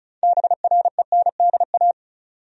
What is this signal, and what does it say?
Morse code, heard as a keyed tone, about 34 wpm, 700 Hz. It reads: BRENDA
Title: USA Common Names - 35wpm